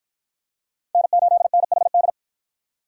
Morse code, heard as a keyed tone, about 52 wpm, 700 Hz. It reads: N8NHD